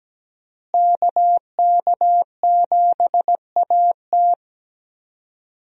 Morse code, heard as a keyed tone, about 17 wpm, 700 Hz. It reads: KK7AT
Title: USA Call Signs - 5 Characters - Repeated 3x - 17wpm